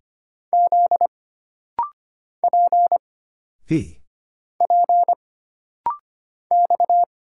Morse code, heard as a keyed tone, about 25 wpm, 700 Hz. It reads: Z P P X
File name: Single Character (Letter:Number:Punctuation:Prosign) - Rapid-Fire - 25wpm